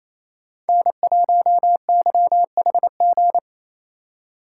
Morse code, heard as a keyed tone, about 28 wpm, 700 Hz. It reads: N1YHG
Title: USA Call Signs - 5 Characters - 2x Character Spacing - 28wpm